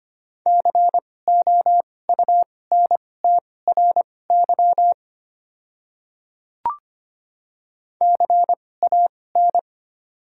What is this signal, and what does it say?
Morse code, heard as a keyed tone, about 25 wpm, spaced as for 18 wpm, 690 Hz. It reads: COUNTRY CAN
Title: 2x Character Spacing - Top 300 Words - 25wpm